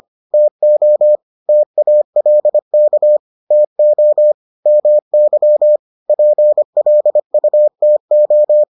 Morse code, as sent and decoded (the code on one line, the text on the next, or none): - --- / - .- .-.. -.- / - --- / -- -.-- / .--. .-.. ..- - ---
TO TALK TO MY PLUTO